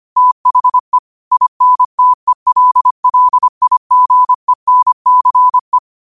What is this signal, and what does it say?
Morse code, heard as a keyed tone, about 25 wpm, 1000 Hz. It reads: THE INTELLIGENCE
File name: qotdfiles / 25wpm